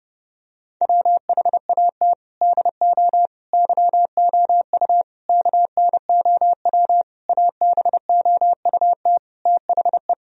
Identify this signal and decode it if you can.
Morse code, heard as a keyed tone, about 29 wpm, 700 Hz. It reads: WHAT DO YOU KNOW ABOUT THE
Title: Sentences from Top 1000 Words - Review Part 3 - 30wpm